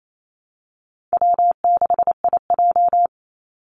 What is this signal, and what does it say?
Morse code, heard as a keyed tone, about 28 wpm, 700 Hz. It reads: W6IJ
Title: Easy Call Signs - 28wpm